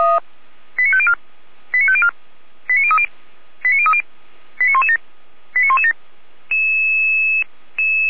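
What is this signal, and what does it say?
Five-tone selective call (ZVEI), heard as quick runs of five tones, tones of about 70 ms. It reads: 89504 89504 89030 89030 89117 89117
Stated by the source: ZVEI five-tone selective call received by SDR.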